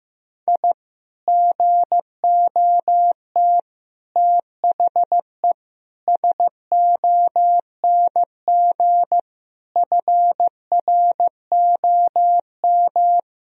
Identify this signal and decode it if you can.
Morse code, heard as a keyed tone, about 15 wpm, 700 Hz. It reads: I GOT THE SONG FROM
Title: Sentences from Top 1000 Words - Review Part 2 - 15wpm